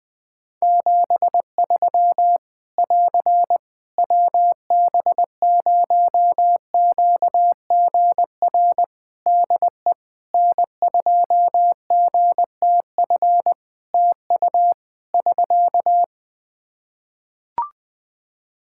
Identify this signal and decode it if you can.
Morse code, heard as a keyed tone, about 20 wpm, 700 Hz. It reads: 73 <AR> WB0QGR DE N2GTF TU <SK>
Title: Practice QSO - Exchange 4 - Ending - 20wpm